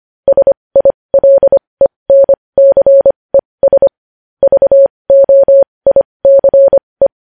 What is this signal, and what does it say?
Morse code, heard as a keyed tone, about 25 wpm, spaced as for 20 wpm, 550 Hz. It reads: SILENCES VOICE
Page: News Headlines in Morse Code at 20 WPM